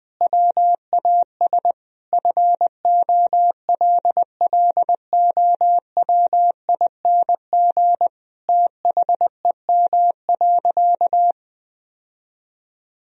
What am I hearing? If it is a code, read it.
Morse code, heard as a keyed tone, about 20 wpm, 700 Hz. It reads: WAS FOLLOWING THEM.
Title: Sentences from Top 1000 Words - Review Part 4 - 20wpm